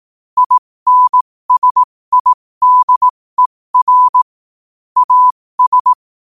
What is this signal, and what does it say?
Morse code, heard as a keyed tone, about 18 wpm, spaced as for 15 wpm, 1000 Hz. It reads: INSIDER AS
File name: qotdfiles / 15wpm